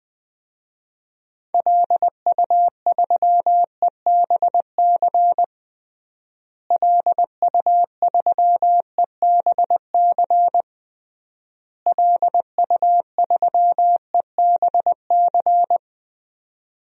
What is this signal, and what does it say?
Morse code, heard as a keyed tone, about 20 wpm, 700 Hz. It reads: LU3EBC LU3EBC LU3EBC
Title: DXCC-100 Call Signs - 6 Characters - Repeated 3x - 20wpm